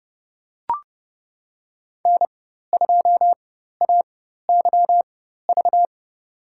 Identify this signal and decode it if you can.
Morse code, heard as a keyed tone, about 31 wpm, spaced as for 14 wpm, 700 Hz. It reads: N2AYV
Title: USA Call Signs - 4x Character Spacing - 30wpm